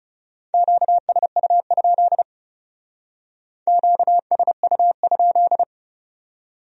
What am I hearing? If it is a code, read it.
Morse code, heard as a keyed tone, about 37 wpm, 700 Hz. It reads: QSU? QSU?
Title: Speed-Racing Q-Codes - 30wpm